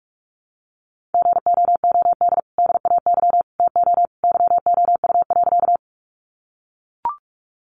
Morse code, heard as a keyed tone, about 44 wpm, 700 Hz. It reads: GOOD DAY TO YOU.
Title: Sentences from Top 200 Words 45wpm